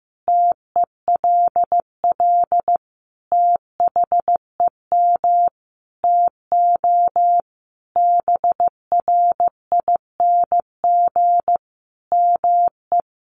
Morse code, heard as a keyed tone, about 15 wpm, 700 Hz. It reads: TELL THEM TO BRING ME
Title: Sentences from Top 500 Words 15wpm